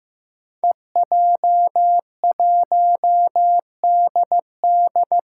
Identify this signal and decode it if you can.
Morse code, heard as a keyed tone, about 15 wpm, 700 Hz. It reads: EJ1DD
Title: Five Challenging Letter-Number Combinations 15wpm